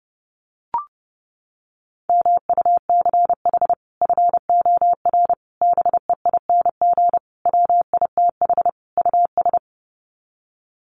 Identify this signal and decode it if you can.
Morse code, heard as a keyed tone, about 30 wpm, 700 Hz. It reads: MUCH FOR BEING WITH US
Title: Sets of 5 Words 30wpm